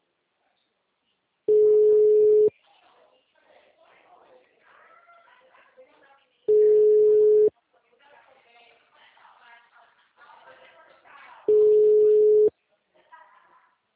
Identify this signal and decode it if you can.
Ringback tone (Europe, 425 Hz), with speech.